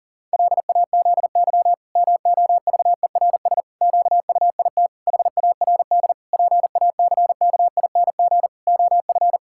Morse code, heard as a keyed tone, about 40 wpm, 700 Hz. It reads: LAZY MOVERS QUIT HARD PACKING OF
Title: Pangram Challenge v2 - 40wpm